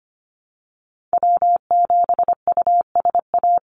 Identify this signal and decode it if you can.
Morse code, heard as a keyed tone, about 25 wpm, 700 Hz. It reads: W7USA
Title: NA SOTA Activator Callsigns 25wpm